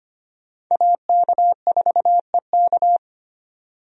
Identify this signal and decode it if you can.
Morse code, heard as a keyed tone, about 25 wpm, 700 Hz. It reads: AK4EK
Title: USA Call Signs - 5 Characters - 25wpm